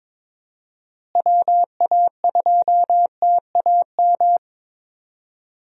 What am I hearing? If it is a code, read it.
Morse code, heard as a keyed tone, about 22 wpm, 700 Hz. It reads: WA2TAM
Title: USA Call Signs - 6 Characters - Repeated 3x - 22wpm